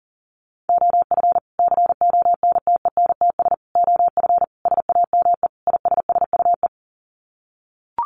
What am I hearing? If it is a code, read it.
Morse code, heard as a keyed tone, about 40 wpm, 700 Hz. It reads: OF CONTENTS OF SAME ISSUE
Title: Sets of 5 Words Encore2 40wpm